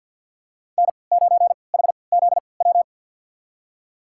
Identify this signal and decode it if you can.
Morse code, heard as a keyed tone, about 52 wpm, spaced as for 29 wpm, 690 Hz. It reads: N9HZW
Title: USA Call Signs - 5 Characters - 3x Character Spacing - 50wpm